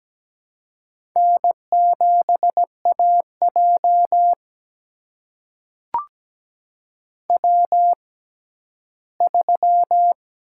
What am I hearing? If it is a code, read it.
Morse code, heard as a keyed tone, about 17 wpm, spaced as for 11 wpm, 700 Hz. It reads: N7AJ W3
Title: USA Call Signs - 6x Character Spacing - 17wpm